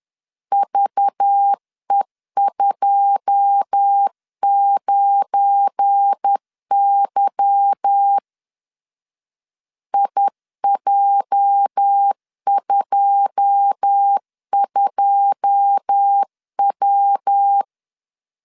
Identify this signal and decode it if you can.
Morse code, heard as a keyed tone, about 10 wpm, 800 Hz. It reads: VE29Y IJ22W
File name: LEZIONI / 9 2 10wpm 2736